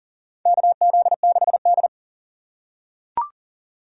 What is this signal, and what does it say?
Morse code, heard as a keyed tone, about 40 wpm, 700 Hz. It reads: KZ6D
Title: USA Call Signs - 4 Characters - 4x Character Spacing - 40wpm